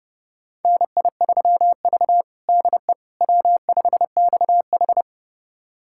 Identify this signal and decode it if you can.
Morse code, heard as a keyed tone, about 30 wpm, 700 Hz. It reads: NI3V DE W5XH
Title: USA Callsign DE Callsign - 4 Characters - 30wpm